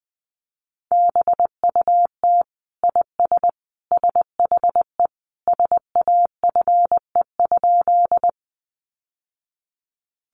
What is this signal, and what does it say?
Morse code, heard as a keyed tone, about 20 wpm, 700 Hz. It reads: BUT IS SHE SAFE?
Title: Sentences from Top 900 Words 20wpm